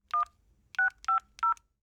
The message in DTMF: 1 650